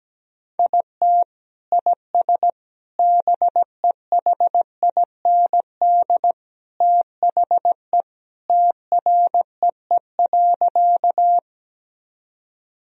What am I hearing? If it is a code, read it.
Morse code, heard as a keyed tone, about 17 wpm, 700 Hz. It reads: IT IS BEHIND THE TREE.
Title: Sentences from Top 1000 Words - Review Part 1 - 17wpm